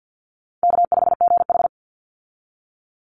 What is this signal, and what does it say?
Morse code, heard as a keyed tone, about 50 wpm, 700 Hz. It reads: K5GH
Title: Easy Call Signs - 50wpm